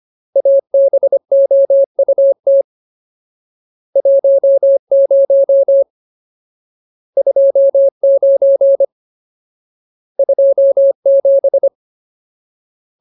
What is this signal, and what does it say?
Morse code, heard as a keyed tone, about 25 wpm, 550 Hz. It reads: ABOUT 10 29 27
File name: Ultimate Word List / 25WPM Ultimate